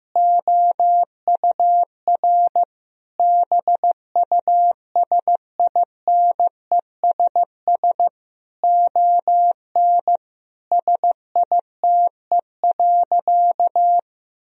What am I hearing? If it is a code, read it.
Morse code, heard as a keyed tone, about 15 wpm, 700 Hz. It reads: OUR BUSINESS ON SITE.
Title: Sentences from Top 1000 Words - Review Part 4 - 15wpm